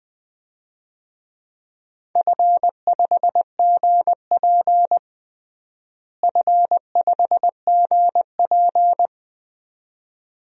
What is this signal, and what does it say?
Morse code, heard as a keyed tone, about 20 wpm, 700 Hz. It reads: F5GP F5GP
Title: DXCC-100 Call Signs - 4 Characters - Repeated 2x - 20wpm